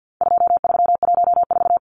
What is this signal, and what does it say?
Morse code, heard as a keyed tone, about 50 wpm, 700 Hz. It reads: 2314
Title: Numbers - 4-Digit 50wpm